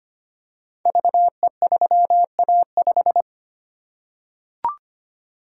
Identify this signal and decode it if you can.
Morse code, heard as a keyed tone, about 25 wpm, 700 Hz. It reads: VE3A5
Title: Code Groups - 5 Letter:Numbers - 25wpm